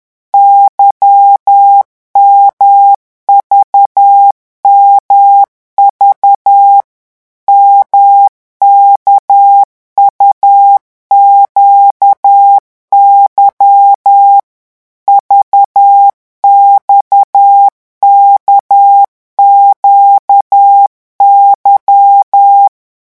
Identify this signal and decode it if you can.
Morse code, heard as a keyed tone, about 10 wpm, 800 Hz. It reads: YMVMV MKUQY VXKQY
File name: morse mp3 / revision2-vitesse 10 mots